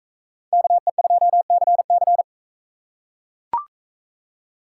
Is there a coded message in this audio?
Yes: Morse code from 0.5 to 2.2 s, heard as a keyed tone, about 38 wpm, 700 Hz. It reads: KE2CC